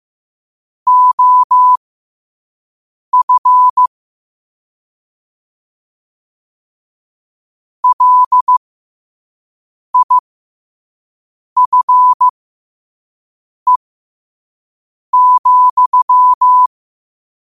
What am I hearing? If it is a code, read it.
Morse code, heard as a keyed tone, about 15 wpm, spaced as for 5 wpm, 1000 Hz. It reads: OF LIFE,
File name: qotdfiles / 5wpm